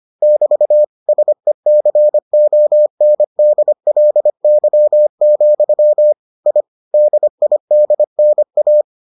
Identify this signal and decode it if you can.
Morse code, heard as a keyed tone, about 25 wpm, 600 Hz. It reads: <BT> SECONDLY, I DIDNA